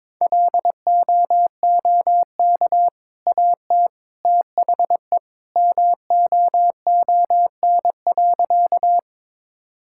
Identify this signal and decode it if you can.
Morse code, heard as a keyed tone, about 22 wpm, 690 Hz. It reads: LOOK AT THE MOON.
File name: practice / 22wpm / Easy Sentences with Hints - 22wpm